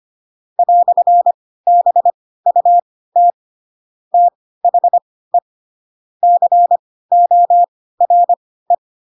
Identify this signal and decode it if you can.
Morse code, heard as a keyed tone, about 25 wpm, spaced as for 16 wpm, 700 Hz. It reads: "BUT THE CORE
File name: Pinocchio / Pinocchio ch7 10 25 ik1hge